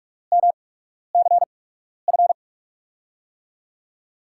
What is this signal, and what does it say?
Morse code, heard as a keyed tone, about 47 wpm, spaced as for 12 wpm, 700 Hz. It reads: MCF
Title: USA Call Signs - 5 Characters - 8x Character Spacing - 45wpm